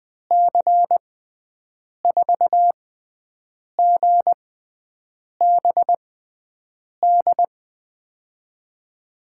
Morse code, heard as a keyed tone, about 21 wpm, spaced as for 7 wpm, 700 Hz. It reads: C4GBD
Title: USA Call Signs - 6 Characters - 6x Character Spacing - 20wpm